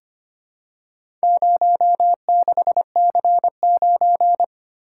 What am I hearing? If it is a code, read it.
Morse code, heard as a keyed tone, about 25 wpm, 700 Hz. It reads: 06C9
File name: practice / 25wpm / Four Letter-Number Combinations - 25wpm